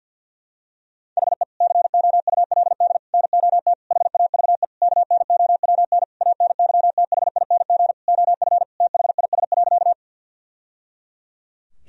Morse code, heard as a keyed tone, about 50 wpm, 700 Hz. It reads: HE COULD NOT HAVE KNOWN ANYTHING OF THIS.
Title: Sentences from Top 1000 Words - Review Part 3 - 50wpm